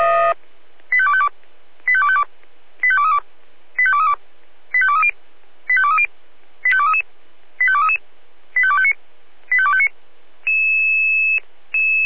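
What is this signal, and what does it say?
Five-tone selective call (ZVEI), heard as quick runs of five tones, tones of about 70 ms. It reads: 85252 85252 85232 85232 85230 85230 85233 85233 85259 85259
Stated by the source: Five-tone ZVEI selective call 25233, received on SDR.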